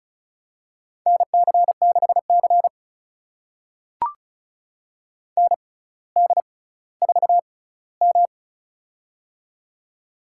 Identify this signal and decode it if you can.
Morse code, heard as a keyed tone, about 36 wpm, spaced as for 18 wpm, 700 Hz. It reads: NC6C ND4M
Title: USA Call Signs - 6x Character Spacing - 35wpm